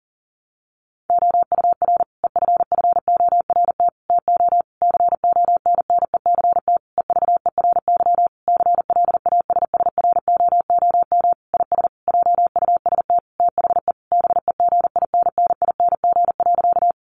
Morse code, heard as a keyed tone, about 40 wpm, 700 Hz. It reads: OUR EFFORT TO CONNECT EVERY CLASSROOM IS JUST THE BEGINNING.
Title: Sentences from Top 1500 Words 40wpm